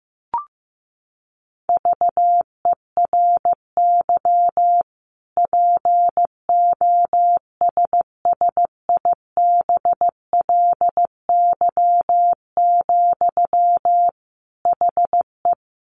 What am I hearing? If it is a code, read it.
Morse code, heard as a keyed tone, about 15 wpm, 700 Hz. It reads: VERY POSSIBLY, HE
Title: Sentences from Top 1300 Words 15wpm